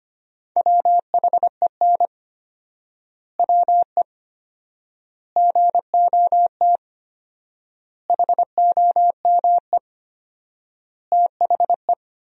Morse code, heard as a keyed tone, about 25 wpm, 700 Hz. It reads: WHEN WE GOT HOME THE
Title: 4x Word-Spacing - Sentences from Top 500 Words - 25wpm